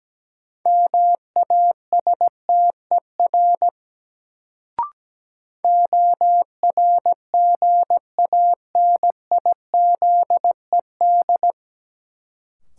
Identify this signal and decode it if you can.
Morse code, heard as a keyed tone, about 17 wpm, 700 Hz. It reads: MASTER ORGANIZED
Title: Sentences from Top 1300 Words - New Word Drill - 17wpm